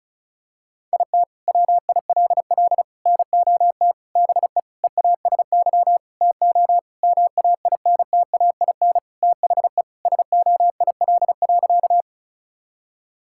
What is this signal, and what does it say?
Morse code, heard as a keyed tone, about 36 wpm, 700 Hz. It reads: IT WILL NOT BE EASY TO MAINTAIN THE SOIL.
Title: Sentences from Top 1000 Words - Review Part 5 - 35wpm